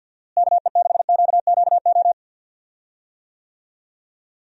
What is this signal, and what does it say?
Morse code, heard as a keyed tone, about 53 wpm, 700 Hz. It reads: KE6QQO
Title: USA Call Signs - 6 Characters - Repeated 3x - 50wpm